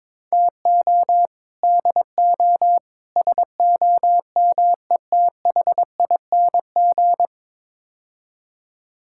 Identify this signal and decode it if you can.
Morse code, heard as a keyed tone, about 21 wpm, 700 Hz. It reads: TO DO SOMETHING
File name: practice / 22wpm / Build A Phrase - 5-Word Phrases - 22wpm